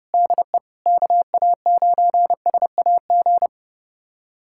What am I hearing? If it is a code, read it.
Morse code, heard as a keyed tone, about 30 wpm, 700 Hz. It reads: DE KA9SAG
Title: USA Callsign DE Callsign - 6 Characters - 30wpm